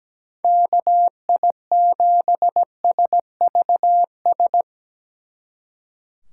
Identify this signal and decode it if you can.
Morse code, heard as a keyed tone, about 18 wpm, 700 Hz. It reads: KI7SVS
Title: Speed Racing - USA 2x3 Call Signs 17wpm